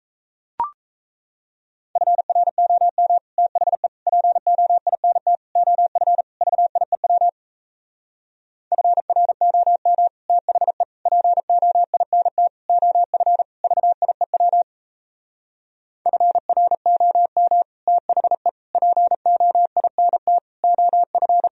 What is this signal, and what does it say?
Morse code, heard as a keyed tone, about 37 wpm, 700 Hz. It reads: FROM THE POINT OF VIEW FROM THE POINT OF VIEW FROM THE POINT OF